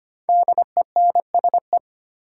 Morse code, heard as a keyed tone, about 25 wpm, 700 Hz. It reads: DENSE
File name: 5 Letter Words - First 3 Letters Given - 25wpm